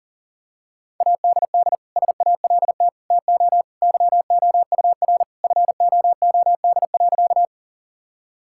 Morse code, heard as a keyed tone, about 37 wpm, 700 Hz. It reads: ADD SALT TO YOUR FOOD.
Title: Sentences from Top 1000 Words - Review Part 5 - 40wpm